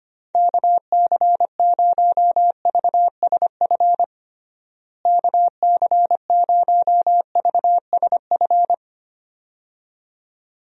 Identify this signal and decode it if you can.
Morse code, heard as a keyed tone, about 25 wpm, 700 Hz. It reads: KC0VSF KC0VSF
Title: USA Call Signs - 6 Characters - Repeated 2x - 25wpm